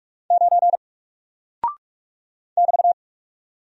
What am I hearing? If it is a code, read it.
Morse code, heard as a keyed tone, about 44 wpm, 700 Hz. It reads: 9 <BT>
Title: Single Character (Letter:Number:Punctuation:Prosign) Alt 45wpm